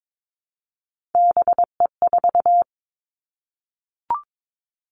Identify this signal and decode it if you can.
Morse code, heard as a keyed tone, about 22 wpm, 700 Hz. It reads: BE4
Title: Three Challenging Letter-Number Combinations 22wpm